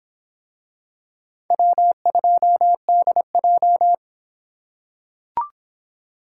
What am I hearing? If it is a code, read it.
Morse code, heard as a keyed tone, about 24 wpm, 700 Hz. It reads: W2DJ